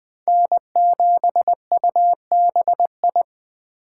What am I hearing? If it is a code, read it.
Morse code, heard as a keyed tone, about 20 wpm, 700 Hz. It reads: N7UBI
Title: Speed Racing - USA 1x3 Call Signs 17wpm